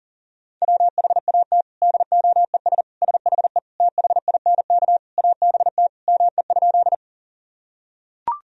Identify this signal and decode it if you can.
Morse code, heard as a keyed tone, about 40 wpm, 700 Hz. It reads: WHAT DOES SHE THINK ABT ME?
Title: Sentences from Top 200 Words and Abbreviations 40wpm